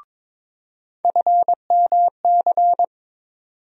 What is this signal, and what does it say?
Morse code, heard as a keyed tone, about 21 wpm, 700 Hz. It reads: FMC
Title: USA Call Sign Postfixes - 3 Letters 22wpm